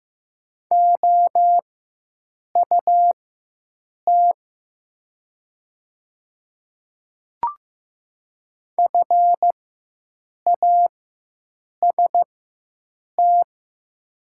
Morse code, heard as a keyed tone, about 15 wpm, spaced as for 7 wpm, 700 Hz. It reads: OUT FAST
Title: 4x Character Spacing - Top 500 Words - 15wpm